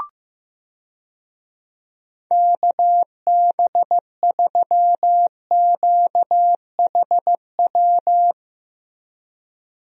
Morse code, heard as a keyed tone, about 15 wpm, 700 Hz. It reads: KB3QHW
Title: USA Call Signs - 6 Characters - Repeated 2x - 15wpm